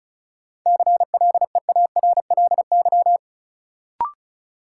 Morse code, heard as a keyed tone, about 36 wpm, 700 Hz. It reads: CLEARLY
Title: Sentences from Top 1300 Words - New Word Drill Alt - 35wpm